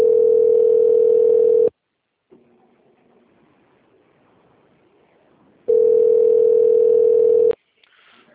Ringback tone (North America), 440 + 480 Hz.
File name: rg-1600-unknown- - - .5480